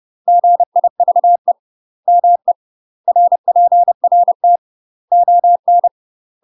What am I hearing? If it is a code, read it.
Morse code, heard as a keyed tone, about 30 wpm, 700 Hz. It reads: GIVE ME RPRT ON
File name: UK MASTER QSOs TRAINING / 30WPM / UK MASTER QSOs 0015